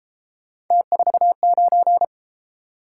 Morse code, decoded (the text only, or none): T49